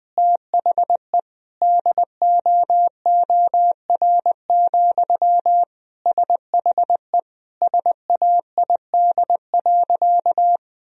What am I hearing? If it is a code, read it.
Morse code, heard as a keyed tone, about 20 wpm, 700 Hz. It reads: THE DOOR, SHE SAID.
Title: Sentences from Top 1000 Words - Review Part 1 - 20wpm